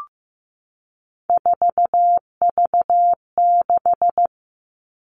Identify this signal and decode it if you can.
Morse code, heard as a keyed tone, about 15 wpm, 700 Hz. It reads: 4V6
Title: All Cut Numbers - 3-Digit 15wpm